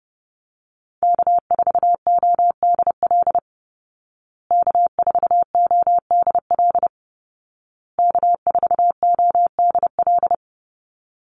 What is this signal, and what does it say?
Morse code, heard as a keyed tone, about 30 wpm, 700 Hz. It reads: K4ODL K4ODL K4ODL
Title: Call Signs Repeated 2x - 30wpm